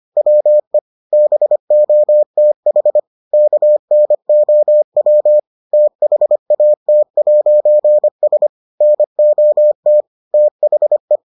Morse code, heard as a keyed tone, about 25 wpm, 600 Hz. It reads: WE BOTH KNOW THAT'S NOT THE